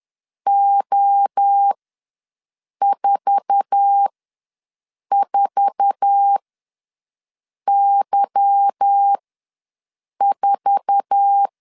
Morse code, heard as a keyed tone, about 10 wpm, spaced as for 6 wpm, 800 Hz. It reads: O44Y4
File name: LEZIONI / 8 YJ45 / 8 YJ4 ws 10wpm 4020